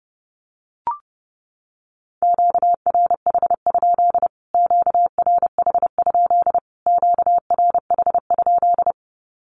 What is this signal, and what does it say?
Morse code, heard as a keyed tone, about 30 wpm, 700 Hz. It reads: QRH? QRH? QRH?
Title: Q-Codes Repeated 3x 30wpm